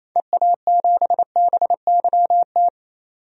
Morse code, heard as a keyed tone, about 28 wpm, 700 Hz. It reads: EA7BYT
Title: DXCC-100 Call Signs - 6 Characters - 28wpm